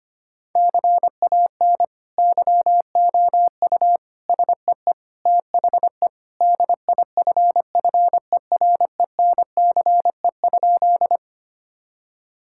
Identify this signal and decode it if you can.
Morse code, heard as a keyed tone, about 25 wpm, 700 Hz. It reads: CAN YOU SEE THE DIFFERENCE?
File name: practice / 25wpm / Sentences with 2-Word Phrases - 25wpm